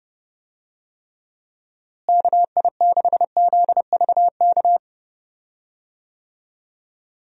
Morse code, heard as a keyed tone, about 32 wpm, 700 Hz. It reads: KI6ZVK
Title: USA Call Signs - 6 Characters - Repeated 3x - 30wpm